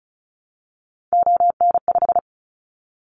Morse code, heard as a keyed tone, about 36 wpm, 700 Hz. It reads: ON5
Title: Callsign Prefixes for Top 50 DXCC - 35wpm